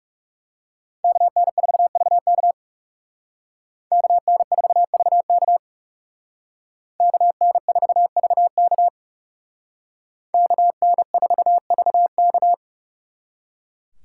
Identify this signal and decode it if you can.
Morse code, heard as a keyed tone, about 39 wpm, 700 Hz. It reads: KN4VK KN4VK KN4VK KN4VK